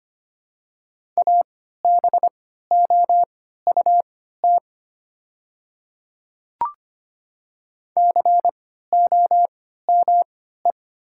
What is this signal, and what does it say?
Morse code, heard as a keyed tone, about 25 wpm, spaced as for 14 wpm, 700 Hz. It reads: ABOUT COME
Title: 3x Character Spacing - Top 300 Words - 25wpm